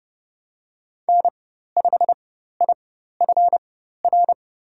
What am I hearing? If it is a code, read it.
Morse code, heard as a keyed tone, about 30 wpm, spaced as for 14 wpm, 700 Hz. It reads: N5IFR
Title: USA Call Signs - 5 Characters - 4x Character Spacing - 30wpm